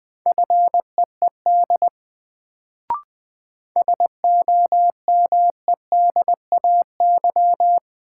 Morse code, heard as a keyed tone, about 20 wpm, 700 Hz. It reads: FEED SOMEDAY
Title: Sentences from Top 1600 Words - New Word Drill - 20wpm